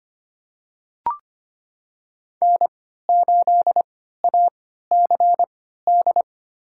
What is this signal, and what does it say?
Morse code, heard as a keyed tone, about 25 wpm, spaced as for 14 wpm, 700 Hz. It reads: N8ACD
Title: USA Call Signs - 3x Character Spacing - 25wpm